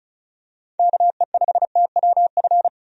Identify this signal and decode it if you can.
Morse code, heard as a keyed tone, about 34 wpm, 700 Hz. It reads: KE5TWF
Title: USA Call Signs - 6 Characters - 35wpm